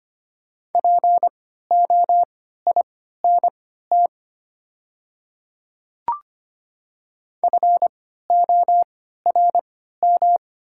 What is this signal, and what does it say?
Morse code, heard as a keyed tone, about 25 wpm, spaced as for 14 wpm, 690 Hz. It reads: POINT FORM
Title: 3x Character Spacing - Top 200 Words - 25wpm